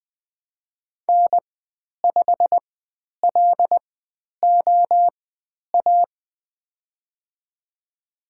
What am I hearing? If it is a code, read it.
Morse code, heard as a keyed tone, about 20 wpm, spaced as for 10 wpm, 700 Hz. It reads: N5LOA
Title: 102 Lesson 17g - Letter u - Call Signs 20wpm